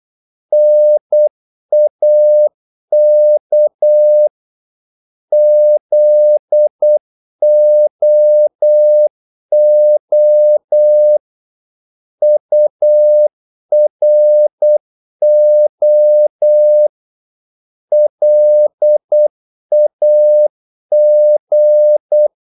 Morse code, heard as a keyed tone, about 8 wpm, 600 Hz. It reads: NAK ZOO URO LAG
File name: Korte ord DK 8wpm